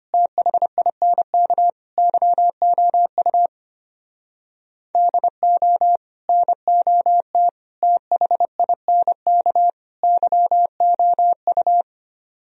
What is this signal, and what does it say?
Morse code, heard as a keyed tone, about 31 wpm, 700 Hz. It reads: THINK YOU DO NOT THINK YOU
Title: Speed-Racing - Sets of 4 Words 25wpm